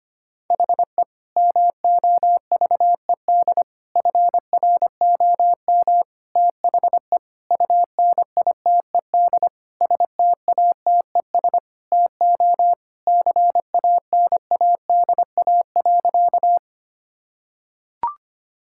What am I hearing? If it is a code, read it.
Morse code, heard as a keyed tone, about 25 wpm, 700 Hz. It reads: HE MOVED FROM THE UNITED STATES TO CANADA.
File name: practice / 25wpm / Sentences with 4-Word Phrases - 25wpm